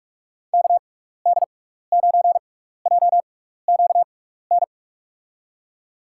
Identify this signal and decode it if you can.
Morse code, heard as a keyed tone, about 46 wpm, spaced as for 15 wpm, 700 Hz. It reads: KD9JQN
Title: USA Call Signs - 6 Characters - 6x Character Spacing - 45wpm